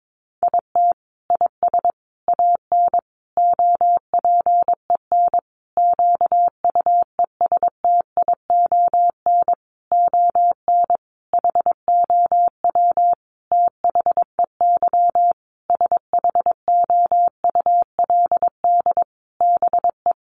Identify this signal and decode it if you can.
Morse code, heard as a keyed tone, about 22 wpm, 700 Hz. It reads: IT IS AN OPEN QUESTION ON HOW THEY SHOULD BE
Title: Sentences from Top 1300 Words 22wpm